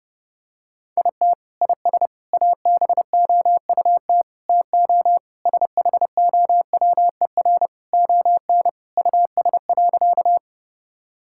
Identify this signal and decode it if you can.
Morse code, heard as a keyed tone, about 30 wpm, 700 Hz. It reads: IT IS ABOUT TO SHOWER ON US.
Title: Sentences from Top 1600 Words 30wpm